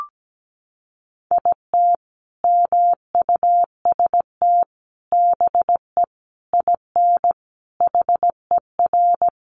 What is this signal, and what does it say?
Morse code, heard as a keyed tone, about 17 wpm, 700 Hz. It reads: IT MUST BE IN HER
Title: Sentences from Top 300 Words 17wpm